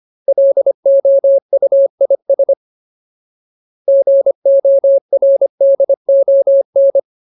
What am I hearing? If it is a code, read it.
Morse code, heard as a keyed tone, about 25 wpm, 550 Hz. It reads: LOUIS GORDON
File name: Ultimate Word List / 25WPM Ultimate